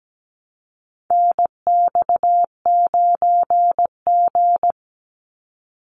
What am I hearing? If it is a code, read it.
Morse code, heard as a keyed tone, about 17 wpm, 700 Hz. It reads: NX9G
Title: Call Signs - 17wpm